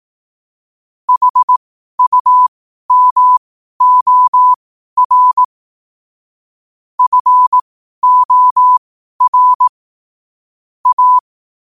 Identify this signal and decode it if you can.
Morse code, heard as a keyed tone, about 18 wpm, spaced as for 12 wpm, 1000 Hz. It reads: HUMOR FOR A
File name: qotdfiles / 12wpm